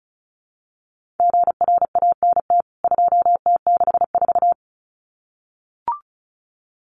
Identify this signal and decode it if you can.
Morse code, heard as a keyed tone, about 34 wpm, 700 Hz. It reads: GRANT 2T64
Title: CWT Exchange 35wpm